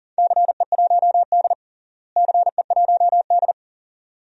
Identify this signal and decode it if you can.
Morse code, heard as a keyed tone, about 40 wpm, 700 Hz. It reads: CE1D CE1D
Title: DXCC-100 Call Signs - 4 Characters - Repeated 2x - 40wpm